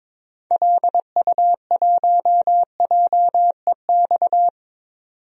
Morse code, heard as a keyed tone, about 22 wpm, 700 Hz. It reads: LU1JEX
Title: DXCC-100 Call Signs - 6 Characters - 22wpm